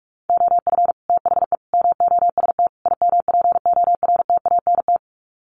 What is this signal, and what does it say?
Morse code, heard as a keyed tone, about 44 wpm, 700 Hz. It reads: OF THE MOST IMPORTANT
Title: Sets of 4 Words 45wpm